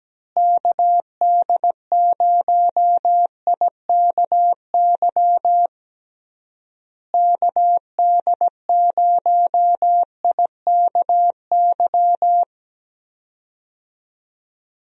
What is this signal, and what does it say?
Morse code, heard as a keyed tone, about 17 wpm, 700 Hz. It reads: KD0IKY KD0IKY
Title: USA Call Signs - 6 Characters - Repeated 3x - 17wpm